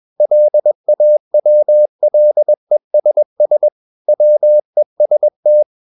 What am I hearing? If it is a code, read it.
Morse code, heard as a keyed tone, about 21 wpm, 600 Hz. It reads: LAWLESS WEST
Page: CW med Gnister